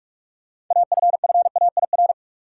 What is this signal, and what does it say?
Morse code, heard as a keyed tone, about 43 wpm, 700 Hz. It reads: AFFAIR
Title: Six Letter Words - 45wpm